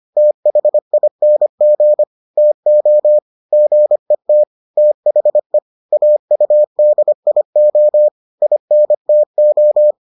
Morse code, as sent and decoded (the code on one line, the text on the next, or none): - .... .. -. --. / - --- / --. - / - .... / .- ..- -.. .. --- / .. -. - ---
THING TO GET THE AUDIO INTO